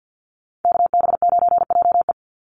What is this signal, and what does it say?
Morse code, heard as a keyed tone, about 50 wpm, 700 Hz. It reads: KB9JE
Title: USA Call Signs 50wpm